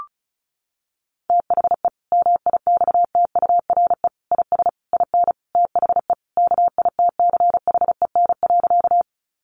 Morse code, heard as a keyed tone, about 34 wpm, 700 Hz. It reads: THE MIXTURE IS IN THE KITCHEN.
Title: Sentences from Top 1500 Words 35wpm